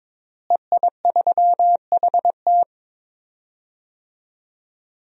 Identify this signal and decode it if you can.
Morse code, heard as a keyed tone, about 22 wpm, 700 Hz. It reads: EI3HT
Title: DXCC-100 Call Signs - 5 Characters - Repeated 3x - 22wpm